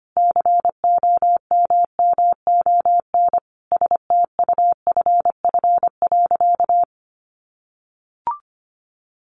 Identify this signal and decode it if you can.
Morse code, heard as a keyed tone, about 25 wpm, 700 Hz. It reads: COMMON STUFF.
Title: Sentences from Top 1400 Words 25wpm